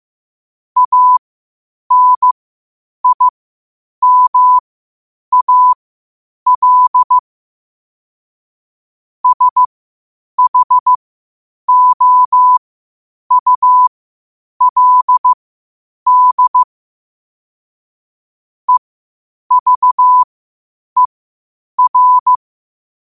Morse code, heard as a keyed tone, about 15 wpm, spaced as for 8 wpm, 1000 Hz. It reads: ANIMAL SHOULD EVER
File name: qotdfiles / 8wpm